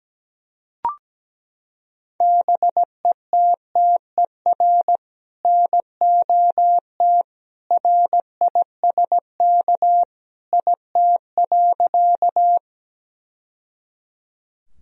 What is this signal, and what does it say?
Morse code, heard as a keyed tone, about 17 wpm, 700 Hz. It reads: BETTER NOT RISK IT.
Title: Sentences from Top 1000 Words - Review Part 4 - 17wpm